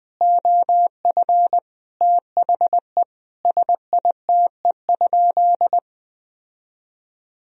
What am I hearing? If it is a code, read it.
Morse code, heard as a keyed tone, about 20 wpm, 700 Hz. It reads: OF THE SITE?
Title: Sentences from Top 1000 Words - Review Part 4 - 20wpm